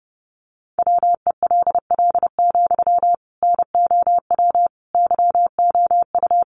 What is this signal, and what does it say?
Morse code, heard as a keyed tone, about 30 wpm, 700 Hz. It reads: WELL, NOW YOU
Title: Sentences from Top 600 Words 30wpm